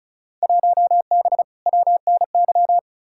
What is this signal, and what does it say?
Morse code, heard as a keyed tone, about 34 wpm, 700 Hz. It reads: 1B WNY
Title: Field Day Exchanges Encore - 35wpm